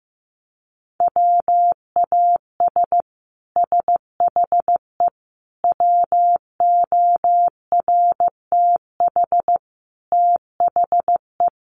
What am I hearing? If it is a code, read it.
Morse code, heard as a keyed tone, about 15 wpm, 700 Hz. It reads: WAS SHE WORTH THE
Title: Sentences from Top 1200 Words 15wpm